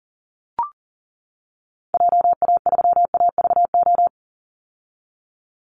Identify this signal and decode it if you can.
Morse code, heard as a keyed tone, about 40 wpm, 700 Hz. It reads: JA3AVO
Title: Call Signs - 40wpm